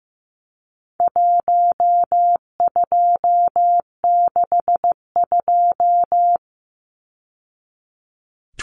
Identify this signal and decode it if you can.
Morse code, heard as a keyed tone, about 15 wpm, 700 Hz. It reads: 1262
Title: Numbers - 4-Digit 15wpm